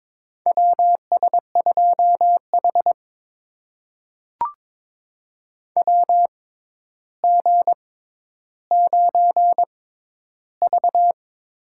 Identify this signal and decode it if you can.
Morse code, heard as a keyed tone, about 23 wpm, spaced as for 11 wpm, 700 Hz. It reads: WS2H WG9V
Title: USA Call Signs - 4 Characters - 6x Character Spacing - 22wpm